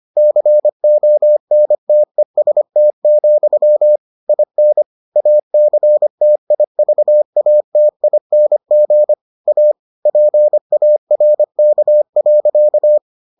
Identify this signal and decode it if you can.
Morse code, heard as a keyed tone, about 25 wpm, 600 Hz. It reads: CONTEST, IN ACTIVATING A PARK.